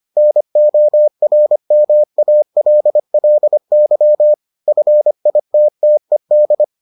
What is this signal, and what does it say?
Morse code, heard as a keyed tone, about 25 wpm, 600 Hz. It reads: NORMALLY FITTED